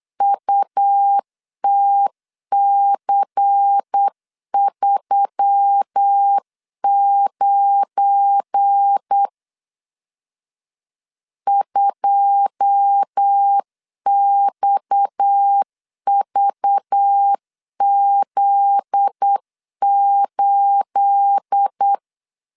Morse code, heard as a keyed tone, about 8 wpm, 800 Hz. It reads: UTC39 2XVZ8
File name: LEZIONI / 9 28 8wpm 3512